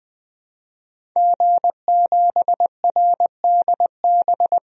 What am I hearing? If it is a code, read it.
Morse code, heard as a keyed tone, about 20 wpm, 700 Hz. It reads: G7RDB
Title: DXCC-100 Call Signs - 5 Characters - 20wpm